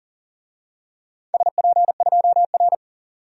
Morse code, heard as a keyed tone, about 41 wpm, 700 Hz. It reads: SP2R
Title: DXCC-100 Call Signs - 4 Characters - 40wpm